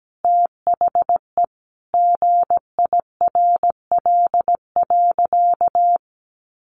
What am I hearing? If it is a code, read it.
Morse code, heard as a keyed tone, about 17 wpm, 700 Hz. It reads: THE GIRL.
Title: Sentences from Top 400 Words 17wpm